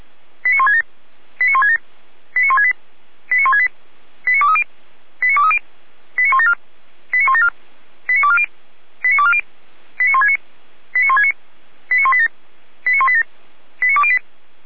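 Five-tone selective call (ZVEI), heard as quick runs of five tones, tones of about 70 ms. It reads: 89167 89167 89168 89168 89230 89230 89174 89174 89240 89240 89169 89169 89187 89187 89208
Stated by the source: ZVEI five-tone selective call 89121 received on SDR.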